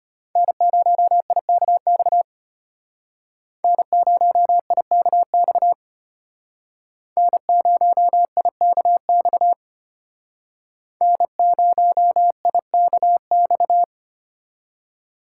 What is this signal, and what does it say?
Morse code, heard as a keyed tone, about 32 wpm, 700 Hz. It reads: N0IKX N0IKX N0IKX N0IKX